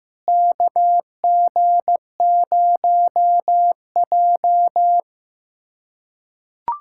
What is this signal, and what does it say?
Morse code, heard as a keyed tone, about 15 wpm, 700 Hz. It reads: KG0J
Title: USA Callsign DE Callsign - 4 Characters - 15wpm